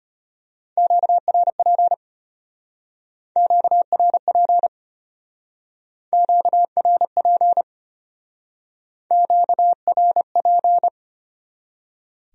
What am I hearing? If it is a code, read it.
Morse code, heard as a keyed tone, about 31 wpm, 700 Hz. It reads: QRP QRP QRP QRP
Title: Speed-Racing Q-Codes - 25wpm